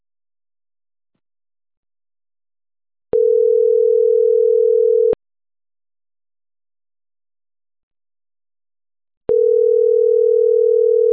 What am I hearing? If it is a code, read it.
Ringback tone (North America), 440 + 480 Hz.